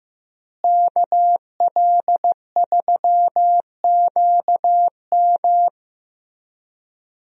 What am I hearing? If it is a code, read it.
Morse code, heard as a keyed tone, about 15 wpm, 700 Hz. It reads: KL3QM
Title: USA Call Signs - 5 Characters - Repeated 2x - 15wpm